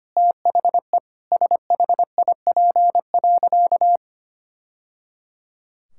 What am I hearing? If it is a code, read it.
Morse code, heard as a keyed tone, about 25 wpm, 700 Hz. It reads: THE SHIP.
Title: Sentences from Top 1000 Words - Review Part 5 - 25wpm